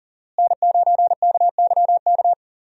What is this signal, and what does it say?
Morse code, heard as a keyed tone, about 40 wpm, 700 Hz. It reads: N9KYK
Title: USA Call Signs - 5 Characters - Repeated 2x - 40wpm